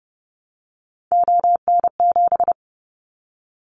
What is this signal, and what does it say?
Morse code, heard as a keyed tone, about 30 wpm, 700 Hz. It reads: ON7
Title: Callsign Prefixes for Top 50 DXCC - 30wpm